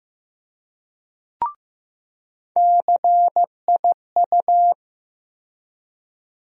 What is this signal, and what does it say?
Morse code, heard as a keyed tone, about 15 wpm, 700 Hz. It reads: CIU
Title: USA Call Sign Postfixes - 3 Letters 15wpm